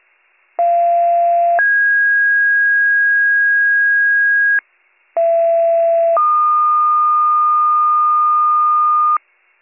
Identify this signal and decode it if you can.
Two-tone page, heard as a one-second tone, then a three-second tone, about 690 then 1650 Hz, about 670 then 1220 Hz, with noise. It is heard from 0.6 to 9.2 s.